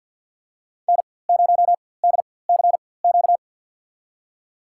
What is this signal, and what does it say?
Morse code, heard as a keyed tone, about 51 wpm, spaced as for 23 wpm, 700 Hz. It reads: N0DCQ
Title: USA Call Signs - 5 Characters - 4x Character Spacing - 50wpm